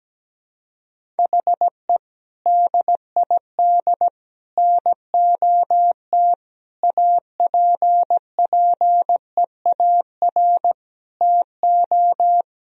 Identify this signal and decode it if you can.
Morse code, heard as a keyed tone, about 17 wpm, 700 Hz. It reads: HE DID NOT APPEAR TO
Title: Sentences from Top 1000 Words - Review Part 2 - 17wpm